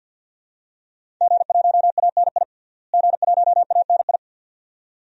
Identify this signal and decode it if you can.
Morse code, heard as a keyed tone, about 51 wpm, 700 Hz. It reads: G1ANI G1ANI
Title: DXCC-100 Call Signs - 5 Characters - Repeated 2x - 50wpm